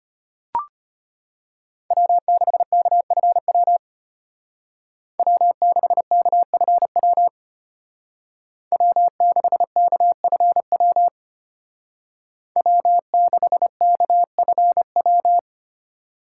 Morse code, heard as a keyed tone, about 30 wpm, 700 Hz. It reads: W6KFW W6KFW W6KFW W6KFW
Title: Speed Racing - USA 1x3 Call Signs 25wpm